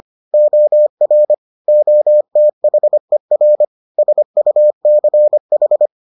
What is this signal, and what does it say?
Morse code, heard as a keyed tone, about 25 wpm, 600 Hz. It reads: OR OTHER SUCH